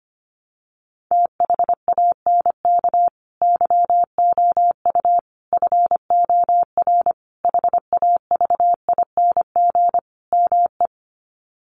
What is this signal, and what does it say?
Morse code, heard as a keyed tone, about 25 wpm, 700 Hz. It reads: THANK YOU FOR HAVING ME
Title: Sets of 5 Words Encore 25wpm